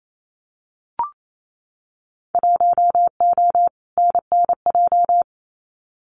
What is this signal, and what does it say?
Morse code, heard as a keyed tone, about 28 wpm, 700 Hz. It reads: 1O NNJ
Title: Winter Field Day Exchanges 28wpm